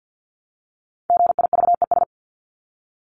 Morse code, heard as a keyed tone, about 50 wpm, 700 Hz. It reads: GIVES
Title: ES:S Suffix - Top 100 Words with Base Words 50wpm